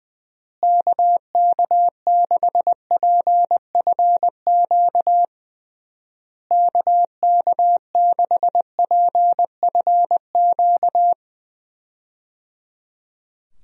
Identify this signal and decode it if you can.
Morse code, heard as a keyed tone, about 20 wpm, 700 Hz. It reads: KK6PFQ KK6PFQ
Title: USA Call Signs - 6 Characters - Repeated 3x - 20wpm